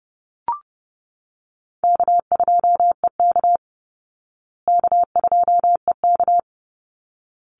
Morse code, heard as a keyed tone, about 29 wpm, 700 Hz. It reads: K2EK K2EK
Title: Easy Call Signs Repeated 2x - 30wpm